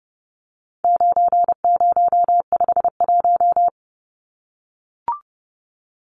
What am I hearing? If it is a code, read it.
Morse code, heard as a keyed tone, about 30 wpm, 700 Hz. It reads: 9051
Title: Numbers - 4-Digit 30wpm